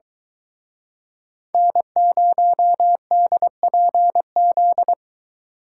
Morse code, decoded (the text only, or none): N0DPZ